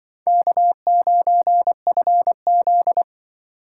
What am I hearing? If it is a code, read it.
Morse code, heard as a keyed tone, about 23 wpm, 700 Hz. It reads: K9FZ
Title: Speed Racing - USA 1x2 Call Signs 20wpm